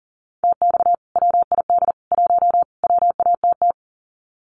Morse code, heard as a keyed tone, about 40 wpm, 700 Hz. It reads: TX WID 1 WATT
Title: Common QSO Phrases 40wpm